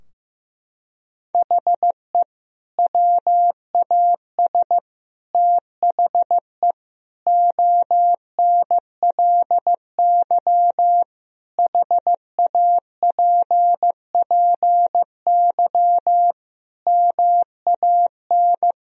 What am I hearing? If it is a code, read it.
Morse code, heard as a keyed tone, about 15 wpm, 700 Hz. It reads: HE WAS THE ONLY HAPPY MAN
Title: Sentences from Top 1000 Words - Review Part 3 - 15wpm